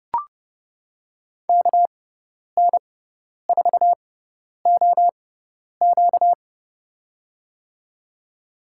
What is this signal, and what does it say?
Morse code, heard as a keyed tone, about 32 wpm, spaced as for 10 wpm, 700 Hz. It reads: KN4OQ
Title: USA Call Signs - 5 Characters - 6x Character Spacing - 30wpm